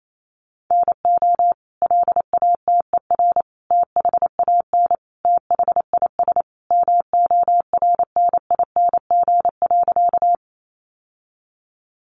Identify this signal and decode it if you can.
Morse code, heard as a keyed tone, about 28 wpm, 700 Hz. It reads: NO LATER THAN THIS MORNING.
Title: Sentences from Top 700 Words 28wpm